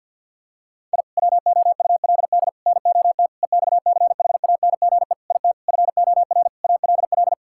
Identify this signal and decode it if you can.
Morse code, heard as a keyed tone, about 50 wpm, 700 Hz. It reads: I WOULD NOT EXCHANGE IT FOR ALL
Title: Sentences from Top 1000 Words - Review Part 6 - 50wpm